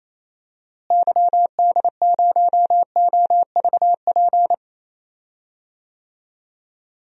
Morse code, heard as a keyed tone, about 28 wpm, 700 Hz. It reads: YD0OVP
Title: DXCC-100 Call Signs - 6 Characters - Repeated 3x - 28wpm